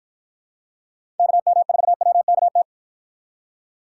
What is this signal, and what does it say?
Morse code, heard as a keyed tone, about 50 wpm, 700 Hz. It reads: KM4WKT